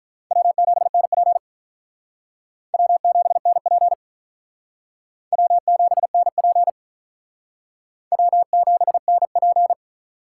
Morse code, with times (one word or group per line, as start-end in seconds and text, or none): 0.3-1.4 W7NP
2.7-3.9 W7NP
5.3-6.7 W7NP
8.1-9.7 W7NP